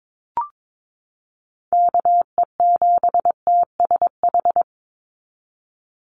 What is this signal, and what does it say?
Morse code, heard as a keyed tone, about 22 wpm, 700 Hz. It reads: KE7TSH
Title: USA Call Signs 22wpm